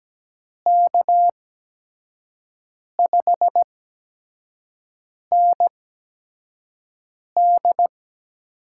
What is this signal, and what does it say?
Morse code, heard as a keyed tone, about 17 wpm, spaced as for 5 wpm, 700 Hz. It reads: K5ND
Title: USA Call Signs - 5 Characters - 8x Character Spacing - 17wpm